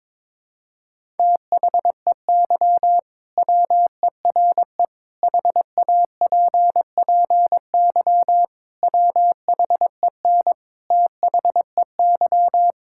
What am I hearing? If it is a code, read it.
Morse code, heard as a keyed tone, about 22 wpm, 700 Hz. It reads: THEY WERE HAPPY WHEN THEY
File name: practice / 22wpm / Sentences with 2-Word Phrases - 22wpm